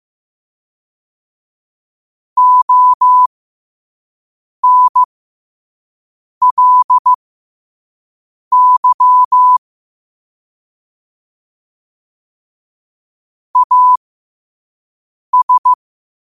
Morse code, heard as a keyed tone, about 15 wpm, spaced as for 5 wpm, 1000 Hz. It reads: ONLY AS
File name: qotdfiles / 5wpm